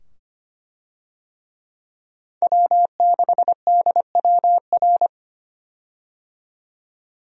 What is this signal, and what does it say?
Morse code, heard as a keyed tone, about 25 wpm, 700 Hz. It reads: W6DWR
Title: USA Call Signs - 5 Characters - 25wpm